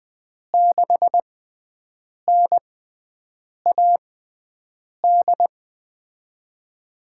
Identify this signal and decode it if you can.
Morse code, heard as a keyed tone, about 20 wpm, spaced as for 7 wpm, 700 Hz. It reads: 6NAD
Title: USA Call Signs - 6 Characters - 6x Character Spacing - 20wpm